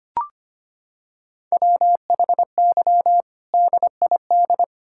Morse code, heard as a keyed tone, about 25 wpm, 700 Hz. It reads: WHY DID
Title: Sentences from Top 1600 Words 25wpm